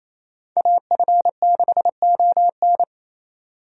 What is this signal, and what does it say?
Morse code, heard as a keyed tone, about 28 wpm, 700 Hz. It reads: AF6ON
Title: USA Call Signs - 5 Characters - 28wpm